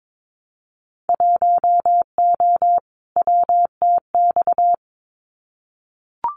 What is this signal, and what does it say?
Morse code, heard as a keyed tone, about 22 wpm, 700 Hz. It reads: 1O WTX
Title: Winter Field Day Exchanges Encore 22wpm